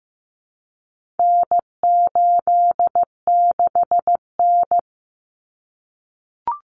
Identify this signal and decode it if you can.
Morse code, heard as a keyed tone, about 15 wpm, 700 Hz. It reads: N86N
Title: Common Cut Numbers - 4-Digit 15wpm